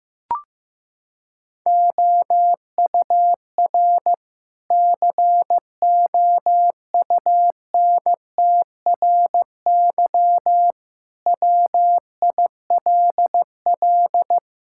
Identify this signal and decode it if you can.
Morse code, heard as a keyed tone, about 15 wpm, 700 Hz. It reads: OUR COUNTRY WILL
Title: Sentences from Top 1000 Words - Review Part 5 - 15wpm